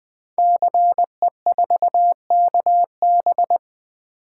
Morse code, heard as a keyed tone, about 20 wpm, 700 Hz. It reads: CE4KB
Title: Code Groups - 5 Letter:Numbers - 20wpm